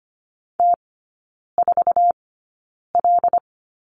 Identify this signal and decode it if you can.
Morse code, heard as a keyed tone, about 25 wpm, spaced as for 9 wpm, 690 Hz. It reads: T4L
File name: Lesson 15e - Number 5 - Three Characters 25wpm